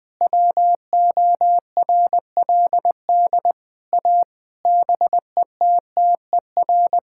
Morse code, heard as a keyed tone, about 21 wpm, 700 Hz. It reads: WORLD A BETTER
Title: Sentences with 7-Word Phrases - 20wpm